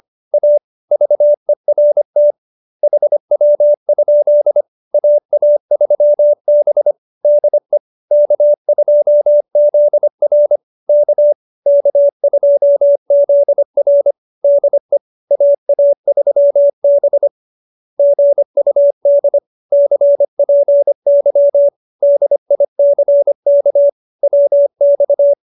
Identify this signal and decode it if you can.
Morse code, heard as a keyed tone, about 25 wpm, 570 Hz. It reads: A VERT HW? AA3B DE K2ZR K K2ZR DE AA3B GUD CPY DICK WX
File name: QSO302 25